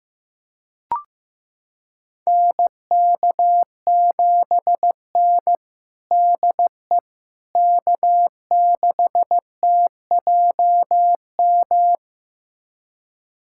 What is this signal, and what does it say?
Morse code, heard as a keyed tone, about 15 wpm, 700 Hz. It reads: NK7N DE K6TJM
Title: USA Callsign DE Callsign 15wpm